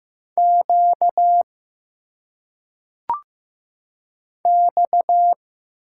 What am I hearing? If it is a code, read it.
Morse code, heard as a keyed tone, about 15 wpm, 700 Hz. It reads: Q X
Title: Single Letters - Phonetic Pronunciation 15wpm